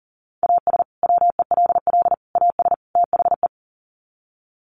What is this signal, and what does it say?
Morse code, heard as a keyed tone, about 40 wpm, 700 Hz. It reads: AS WELL AS THE
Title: Sets of 4 Words 40wpm